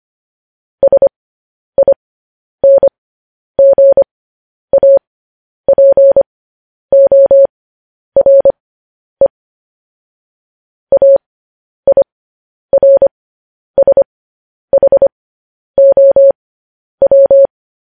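Morse code, heard as a keyed tone, about 25 wpm, spaced as for 10 wpm, 550 Hz. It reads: SINGAPORE AIRSHOW
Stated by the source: Morse code news headlines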